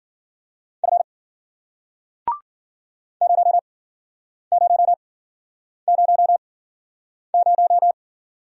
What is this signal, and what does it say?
Morse code, heard as a keyed tone, about 53 wpm, spaced as for 9 wpm, 700 Hz. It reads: F 0000